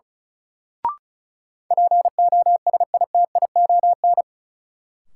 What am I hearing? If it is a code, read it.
Morse code, heard as a keyed tone, about 34 wpm, 700 Hz. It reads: POSITION
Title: Sentences from Top 900 Words - New Word Drill - 35wpm